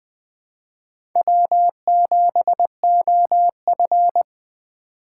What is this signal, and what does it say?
Morse code, heard as a keyed tone, about 20 wpm, 700 Hz. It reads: W7OF
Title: USA Call Signs - 4 Characters - 2x Character Spacing - 20wpm